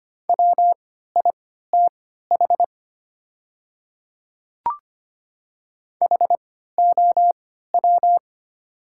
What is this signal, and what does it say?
Morse code, heard as a keyed tone, about 25 wpm, spaced as for 14 wpm, 700 Hz. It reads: WITH HOW
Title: 3x Character Spacing - Top 100 Words - 25wpm